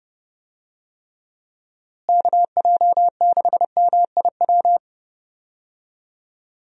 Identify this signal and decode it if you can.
Morse code, heard as a keyed tone, about 30 wpm, 700 Hz. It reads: KJ6MIW
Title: USA Call Signs - 6 Characters - Repeated 2x - 30wpm